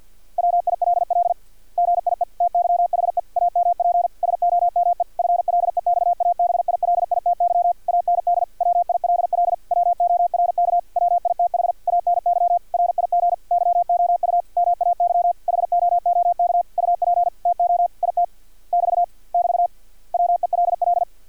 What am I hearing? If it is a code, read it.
Morse code, heard as a keyed tone, about 50 wpm, 700 Hz. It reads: WILL GIE TYHE AMP SOME FLEXABILITY AND WILL WORK WITH ANY RIG YOU MAY HOOK UP TO IT <BT> <BT> WELL
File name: archived / CRSnet